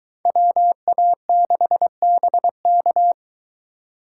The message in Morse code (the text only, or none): WA6BK